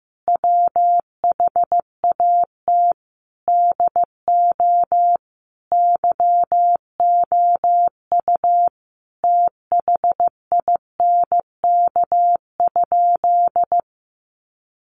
Morse code, heard as a keyed tone, about 15 wpm, 700 Hz. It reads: WHAT DO YOU THINK?
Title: Sentences from Top 200 Words 15wpm